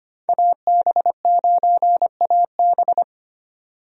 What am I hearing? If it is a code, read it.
Morse code, heard as a keyed tone, about 25 wpm, 700 Hz. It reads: AB9AB
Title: USA Call Signs - 5 Characters - 25wpm